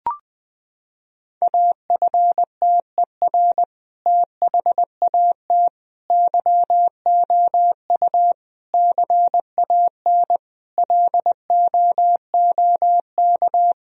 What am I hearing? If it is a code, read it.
Morse code, heard as a keyed tone, about 20 wpm, 700 Hz. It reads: AFTER THAT YOU CAN LOOK
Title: Sentences from Top 1000 Words - Review Part 2 - 20wpm